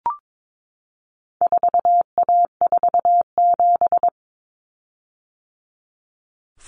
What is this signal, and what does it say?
Morse code, heard as a keyed tone, about 22 wpm, 700 Hz. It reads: 4A47
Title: Common Cut Numbers - 4-Digit 22wpm